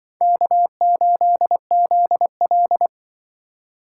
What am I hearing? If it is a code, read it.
Morse code, heard as a keyed tone, about 24 wpm, 700 Hz. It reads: K8ZL
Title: Speed Racing - USA 1x2 Call Signs 20wpm